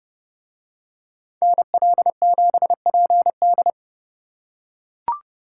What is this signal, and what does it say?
Morse code, heard as a keyed tone, about 27 wpm, 700 Hz. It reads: NL7PD